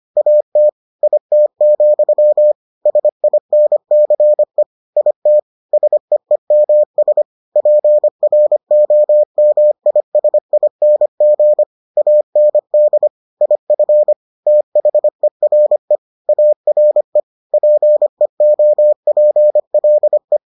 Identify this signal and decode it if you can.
Morse code, heard as a keyed tone, about 25 wpm, 600 Hz. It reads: AT IT, SINCE IT SEEMS PROMISING AND IF THERE ARE PEOPLE